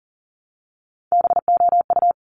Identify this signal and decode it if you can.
Morse code, heard as a keyed tone, about 40 wpm, 700 Hz. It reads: BOU
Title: Top Three Letter Combinations in Words 40wpm